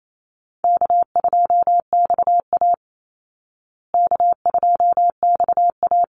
Morse code, heard as a keyed tone, about 28 wpm, 700 Hz. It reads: K2XA K2XA
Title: Easy Call Signs Repeated 3x - 28wpm